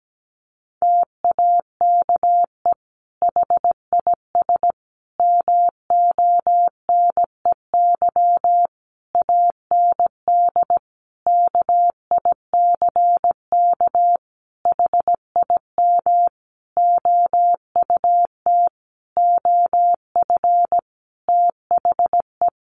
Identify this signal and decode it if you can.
Morse code, heard as a keyed tone, about 17 wpm, 700 Hz. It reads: TAKE HIS MONEY AND KICK HIM OUT OF THE
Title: Sentences from Top 1300 Words 17wpm